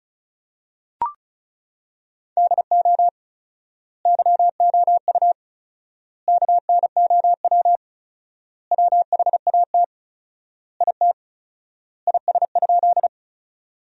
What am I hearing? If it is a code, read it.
Morse code, heard as a keyed tone, about 34 wpm, 700 Hz. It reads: DO YOU KNOW WHAT IT IS?
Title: 4x Word-Spacing - Sentences from Top 100 Words - 35wpm